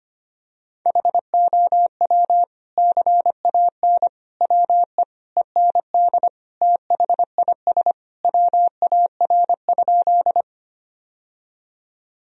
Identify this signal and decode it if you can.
Morse code, heard as a keyed tone, about 25 wpm, 700 Hz. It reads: HOW CAN WE END THIS WAR?
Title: Sentences from Top 1000 Words - Review Part 2 - 25wpm